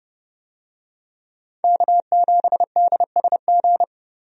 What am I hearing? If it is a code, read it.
Morse code, heard as a keyed tone, about 32 wpm, 700 Hz. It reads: K7DSG
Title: USA Call Signs - 5 Characters - 30wpm